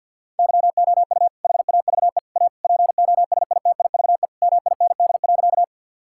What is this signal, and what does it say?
Morse code, heard as a keyed tone, about 50 wpm, 700 Hz. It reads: YOU HAVE A POSITIVE MIND.
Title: Sentences from Top 1000 Words - Review Part 5 - 50wpm